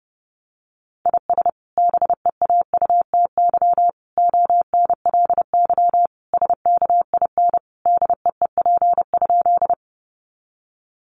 Morse code, heard as a keyed tone, about 30 wpm, 700 Hz. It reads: IS BEAUTY ONLY SKIN DEEP?
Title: Sentences from Top 1300 Words 30wpm